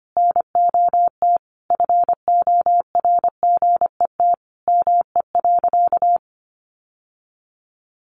Morse code, heard as a keyed tone, about 25 wpm, 700 Hz. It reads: NOT FORGET ME.
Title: Sentences from Top 1200 Words 25wpm